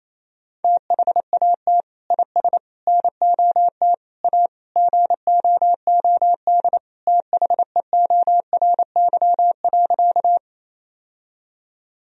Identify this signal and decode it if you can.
Morse code, heard as a keyed tone, about 28 wpm, 700 Hz. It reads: THAT IS NOT A GOOD THEORY.
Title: Sentences from Top 1000 Words - Review Part 3 - 28wpm